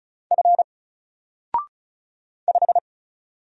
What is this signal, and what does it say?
Morse code, heard as a keyed tone, about 35 wpm, 700 Hz. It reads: F 5
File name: Single-Letter-Number-35wpm